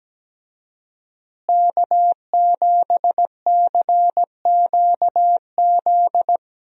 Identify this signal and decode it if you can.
Morse code, heard as a keyed tone, about 17 wpm, 700 Hz. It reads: K7CQZ
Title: USA Call Signs - 5 Characters - Repeated 3x - 17wpm